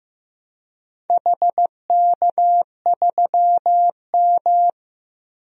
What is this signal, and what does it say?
Morse code, heard as a keyed tone, about 16 wpm, 700 Hz. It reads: HK3M
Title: DXCC-100 Call Signs - 4 Characters - Repeated 3x - 15wpm